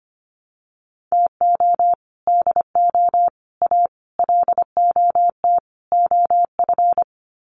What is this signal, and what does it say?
Morse code, heard as a keyed tone, about 25 wpm, 700 Hz. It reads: TO DO A LOT OF
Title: Sets of 5 Words Encore 25wpm